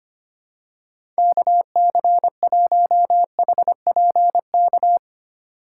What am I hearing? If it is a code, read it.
Morse code, heard as a keyed tone, about 25 wpm, 700 Hz. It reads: KC1HPK
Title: USA Call Signs - 6 Characters - 3x Character Spacing - 25wpm